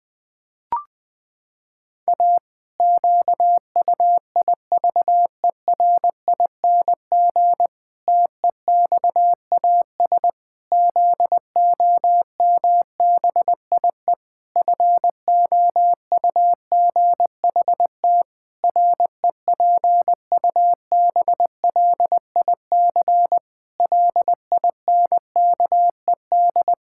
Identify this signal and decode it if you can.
Morse code, heard as a keyed tone, about 21 wpm, 700 Hz. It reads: A QUIVERING TEXAS ZOMBIE FOUGHT REPUBLIC LINKED
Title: Pangram Challenge v2 - 20wpm